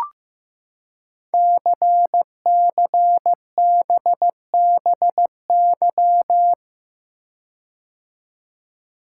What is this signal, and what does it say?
Morse code, heard as a keyed tone, about 15 wpm, 700 Hz. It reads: CCBBY
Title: Code Groups - 5 Letters - 15wpm